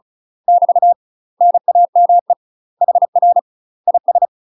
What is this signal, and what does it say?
Morse code, heard as a keyed tone, about 35 wpm, 700 Hz. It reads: <BT> NAME HR IS
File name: UK MASTER QSOs TRAINING / 35WPM / UK MASTER QSOs 0044